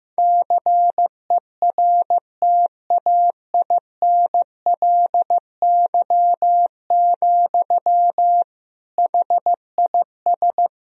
Morse code, heard as a keyed tone, about 16 wpm, 700 Hz. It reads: CERTAINLY, HIS
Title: Sentences from Top 1000 Words - Review Part 3 - 15wpm